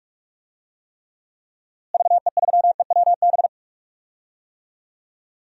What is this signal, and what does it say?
Morse code, heard as a keyed tone, about 46 wpm, 700 Hz. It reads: VE3EWB
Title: DXCC-100 Call Signs - 6 Characters - Repeated 2x - 45wpm